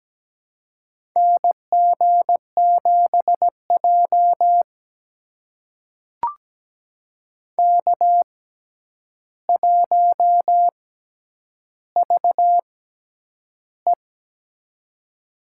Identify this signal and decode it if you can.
Morse code, heard as a keyed tone, about 17 wpm, spaced as for 9 wpm, 700 Hz. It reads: NG7J K1VE
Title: USA Call Signs - 4 Characters - 6x Character Spacing - 17wpm